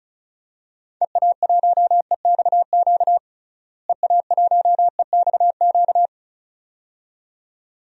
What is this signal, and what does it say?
Morse code, heard as a keyed tone, about 36 wpm, 700 Hz. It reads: EA1EXQ EA1EXQ
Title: DXCC-100 Call Signs - 6 Characters - Repeated 2x - 35wpm